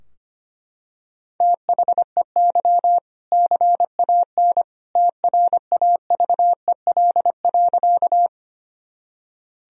Morse code, heard as a keyed tone, about 25 wpm, 700 Hz. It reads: THEY CAN TRAVEL.
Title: Sentences from Top 1000 Words - Review Part 2 - 25wpm